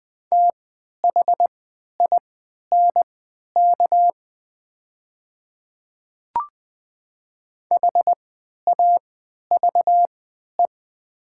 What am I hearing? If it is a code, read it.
Morse code, heard as a keyed tone, about 20 wpm, spaced as for 11 wpm, 700 Hz. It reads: THINK HAVE
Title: 3x Character Spacing - Top 300 Words - 20wpm